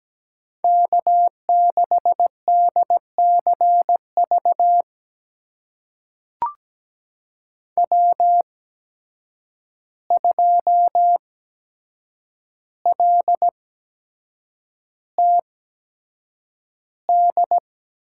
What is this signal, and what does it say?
Morse code, heard as a keyed tone, about 17 wpm, spaced as for 7 wpm, 700 Hz. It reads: K6DCV W2LTD
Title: USA Call Signs - 5 Characters - 8x Character Spacing - 17wpm